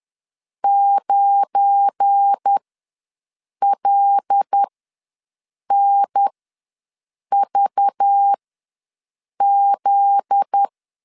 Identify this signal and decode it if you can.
Morse code, heard as a keyed tone, about 10 wpm, spaced as for 6 wpm, 800 Hz. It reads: 9LNVZ
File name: LEZIONI / 8 YJ45 / 8 YJ4 ws 10wpm 4020